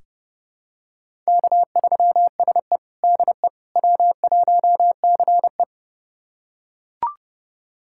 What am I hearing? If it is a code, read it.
Morse code, heard as a keyed tone, about 30 wpm, 700 Hz. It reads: K3SE DE W1CE
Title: USA Callsign DE Callsign - 4 Characters - 30wpm